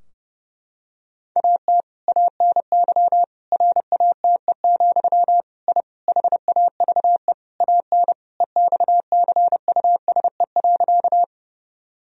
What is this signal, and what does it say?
Morse code, heard as a keyed tone, about 30 wpm, 700 Hz. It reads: AT ANY RATE, I HAVE AN EXCUSE.
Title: Sentences from Top 1000 Words - Review Part 4 - 30wpm